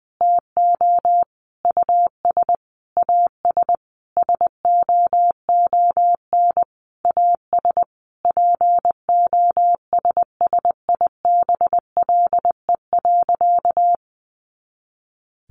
Morse code, heard as a keyed tone, about 20 wpm, 700 Hz. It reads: TO US AS SOON AS POSSIBLE.
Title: Sentences from Top 500 Words 20wpm